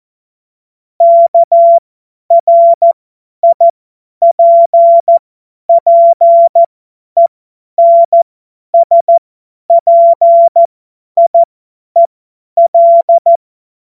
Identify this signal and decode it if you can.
Morse code, heard as a keyed tone, about 14 wpm, spaced as for 10 wpm, 680 Hz. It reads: KRIPPENSPIEL
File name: Morsekurs / Witze / Witz 02. November 2021, 50 F70